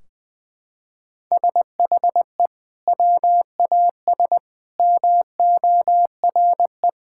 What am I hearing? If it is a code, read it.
Morse code, heard as a keyed tone, about 20 wpm, 700 Hz. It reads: SHE WAS MORE
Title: Sentences from Top 1600 Words 20wpm